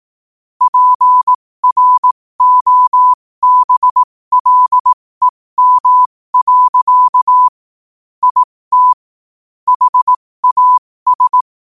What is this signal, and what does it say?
Morse code, heard as a keyed tone, about 18 wpm, spaced as for 15 wpm, 1000 Hz. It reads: PROBLEM. IT HAS
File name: qotdfiles / 15wpm